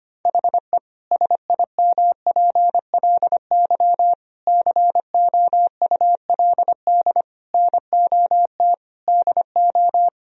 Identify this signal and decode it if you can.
Morse code, heard as a keyed tone, about 25 wpm, 700 Hz. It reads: HE SIMPLY COULD NOT DO
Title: Sentences from Top 1000 Words - Review Part 4 - 25wpm